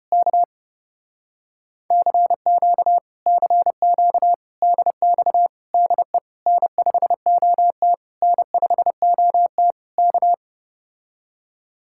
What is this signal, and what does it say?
Morse code, heard as a keyed tone, about 33 wpm, 700 Hz. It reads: K CQ CQ DX DE N5OT N5OT K